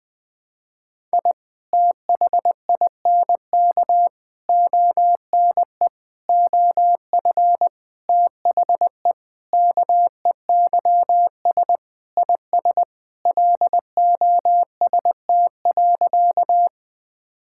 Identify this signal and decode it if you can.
Morse code, heard as a keyed tone, about 20 wpm, 700 Hz. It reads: I THINK ONE OF THE KEYS IS LOST.
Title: Sentences with 5-Word Phrases - 20wpm